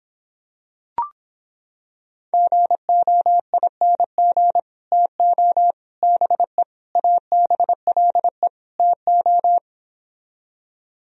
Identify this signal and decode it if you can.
Morse code, heard as a keyed tone, about 26 wpm, 700 Hz. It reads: GOING TO BE ABLE TO